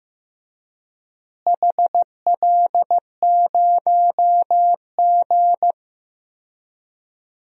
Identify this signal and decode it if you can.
Morse code, heard as a keyed tone, about 15 wpm, 700 Hz. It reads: HL0G
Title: DXCC-100 Call Signs - 4 Characters - Repeated 3x - 15wpm